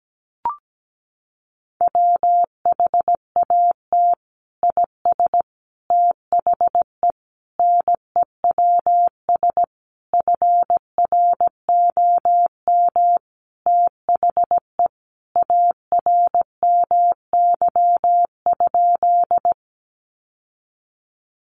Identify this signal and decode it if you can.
Morse code, heard as a keyed tone, about 17 wpm, 700 Hz. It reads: WHAT IS THE NEWS FROM THE ARMY?
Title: Sentences from Top 800 Words 17wpm